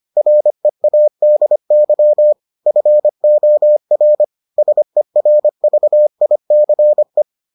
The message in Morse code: READY FOR SERVICE

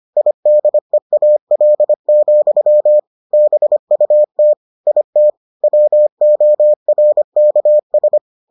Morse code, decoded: IDEAL, BUT IT WORKS

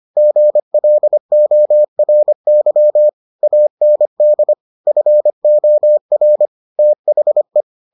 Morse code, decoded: GLORY AND FOR THE

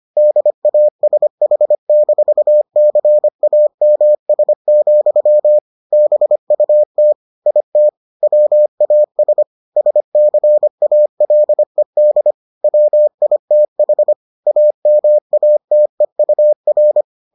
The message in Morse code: DASH-CAMS, BUT IT WAS SCALED WITH AMATEUR

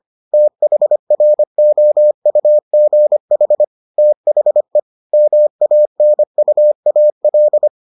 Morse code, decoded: THROUGH THE MANUAL

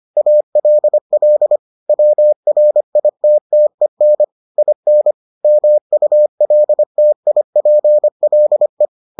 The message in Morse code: ALL WRITTEN IN MULTIPLE

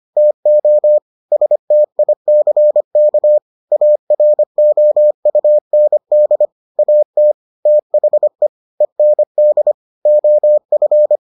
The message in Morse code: TO STICK AROUND AT THE END OF

